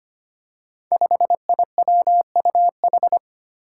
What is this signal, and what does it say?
Morse code, heard as a keyed tone, about 25 wpm, 700 Hz. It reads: 5IWUH
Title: Five Challenging Letter-Number Combinations 25wpm